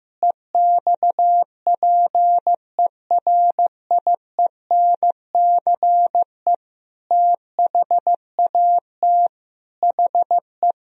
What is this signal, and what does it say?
Morse code, heard as a keyed tone, about 15 wpm, 700 Hz. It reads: EXPERIENCE THAT HE
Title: Sentences from Top 1000 Words - Review Part 3 - 15wpm